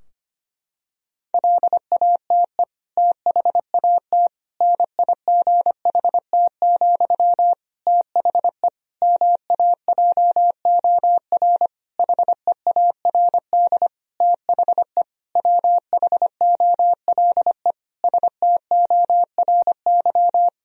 Morse code, heard as a keyed tone, about 25 wpm, 700 Hz. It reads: LATE THAT NIGHT, THE MAJOR HEARD THE WHOLE STORY
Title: Sentences from Top 1000 Words - Review Part 4 - 25wpm